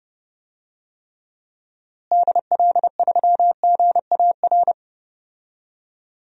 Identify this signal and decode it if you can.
Morse code, heard as a keyed tone, about 32 wpm, 700 Hz. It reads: DL3GAR
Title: DXCC-100 Call Signs - 6 Characters - Repeated 3x - 30wpm